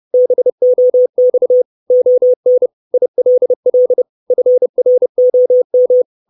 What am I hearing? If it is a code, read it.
Morse code, heard as a keyed tone, about 30 wpm, 500 Hz. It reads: BOX ON ILL FROM
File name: 632 Words 30WPM